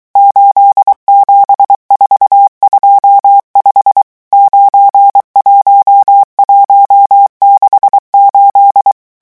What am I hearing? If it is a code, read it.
Morse code, heard as a keyed tone, about 23 wpm, 800 Hz. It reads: 87425 91168
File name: morse mp3 / revision11 chiffres vitesse 22